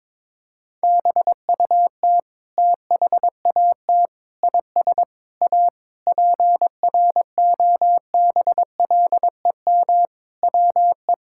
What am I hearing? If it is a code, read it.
Morse code, heard as a keyed tone, about 23 wpm, 700 Hz. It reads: BUT THAT IS A PROBLEM WE
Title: Sentences from Top 1000 Words - Review Part 6 - 22wpm